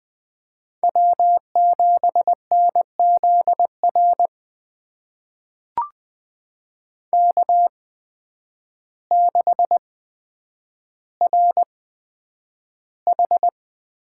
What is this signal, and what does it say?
Morse code, heard as a keyed tone, about 20 wpm, spaced as for 9 wpm, 700 Hz. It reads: W7NZR K6RH
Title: USA Call Signs - 8x Character Spacing - 20wpm